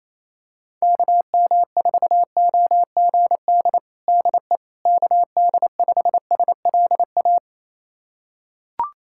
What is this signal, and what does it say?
Morse code, heard as a keyed tone, about 28 wpm, 700 Hz. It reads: KM4OGD DE KD5SLA
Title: USA Callsign DE Callsign - 6 Characters - 28wpm